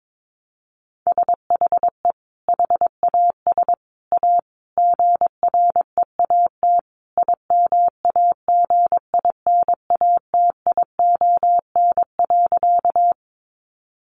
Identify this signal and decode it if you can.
Morse code, heard as a keyed tone, about 22 wpm, 700 Hz. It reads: SHE HAS A GREAT IMAGINATION.
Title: Sentences from Top 1300 Words 22wpm